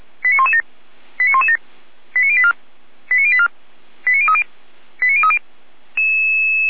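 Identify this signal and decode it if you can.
Five-tone selective call (ZVEI), heard as quick runs of five tones, tones of about 70 ms. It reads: 89117 89117 89084 89084 89030 89030
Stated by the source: ZVEI five-tone selective call received with an SDR.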